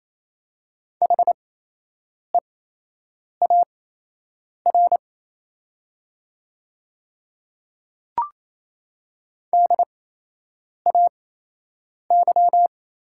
Morse code, heard as a keyed tone, about 28 wpm, spaced as for 8 wpm, 700 Hz. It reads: HEAR DAY
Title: 8x Character Spacing - Top 300 Words - 28wpm